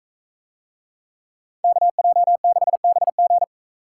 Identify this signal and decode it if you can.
Morse code, heard as a keyed tone, about 41 wpm, 700 Hz. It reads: KJ6BG